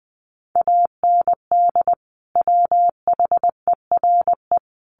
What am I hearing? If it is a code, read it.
Morse code, heard as a keyed tone, about 20 wpm, 700 Hz. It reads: AND WHERE
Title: Sentences from Top 200 Words 20wpm